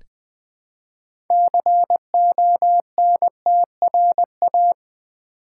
Morse code, heard as a keyed tone, about 20 wpm, 700 Hz. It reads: CONTRA
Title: Build A Word - 3 Letters At A Time - 8-Letter Words - 20wpm